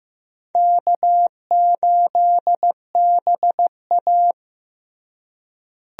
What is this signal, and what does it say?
Morse code, heard as a keyed tone, about 15 wpm, 700 Hz. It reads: K8BA
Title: USA Call Signs - 4 Characters - Repeated 3x - 15wpm